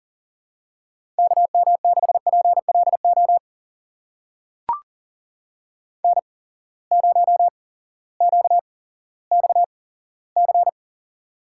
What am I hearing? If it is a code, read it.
Morse code, heard as a keyed tone, about 40 wpm, spaced as for 18 wpm, 700 Hz. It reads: KM6PLO N0QXC
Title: USA Call Signs - 8x Character Spacing - 40wpm